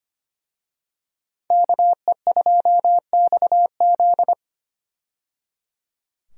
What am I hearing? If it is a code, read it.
Morse code, heard as a keyed tone, about 25 wpm, 690 Hz. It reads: KE2XZ
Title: USA Call Signs - 5 Characters - 25wpm